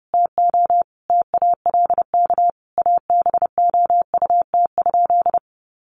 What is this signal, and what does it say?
Morse code, heard as a keyed tone, about 30 wpm, 700 Hz. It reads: TO TALK ABOUT?
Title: Sentences from Top 1100 Words 30wpm